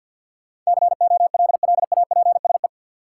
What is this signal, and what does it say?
Morse code, heard as a keyed tone, about 51 wpm, 700 Hz. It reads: COLLAPSE
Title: Sets of 5 Words Encore - New Word Drill - 50wpm